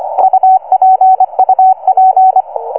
Morse code, heard as a keyed tone, about 24 wpm, 760 Hz, with radio noise. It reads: UP UP